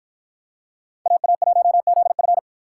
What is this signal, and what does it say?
Morse code, heard as a keyed tone, about 50 wpm, 700 Hz. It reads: AA1ZF